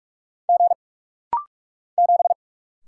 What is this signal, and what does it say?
Morse code, heard as a keyed tone, about 44 wpm, 700 Hz. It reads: G 7
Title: Single Letter-Number - Rapid-Fire 45wpm